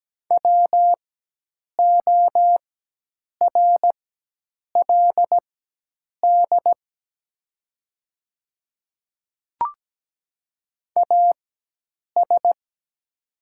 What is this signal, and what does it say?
Morse code, heard as a keyed tone, about 17 wpm, spaced as for 8 wpm, 700 Hz. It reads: WORLD AS
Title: 4x Character Spacing - Top 200 Words - 17wpm